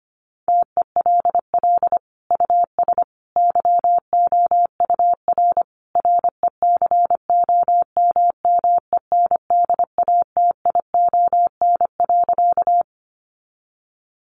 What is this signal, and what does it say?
Morse code, heard as a keyed tone, about 25 wpm, 700 Hz. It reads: TELL US YOUR RECOMMENDATION.
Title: Sentences from Top 1400 Words 25wpm